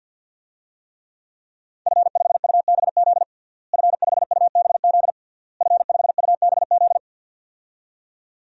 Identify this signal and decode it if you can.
Morse code, heard as a keyed tone, about 52 wpm, 700 Hz. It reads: F5UBZ F5UBZ F5UBZ
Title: DXCC-100 Call Signs - 5 Characters - Repeated 3x - 50wpm